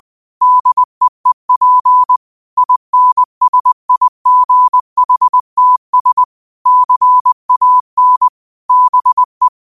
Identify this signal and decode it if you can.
Morse code, heard as a keyed tone, about 20 wpm, 1000 Hz. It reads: DEEP INSIGHTS CAN BE
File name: qotdfiles / 20wpm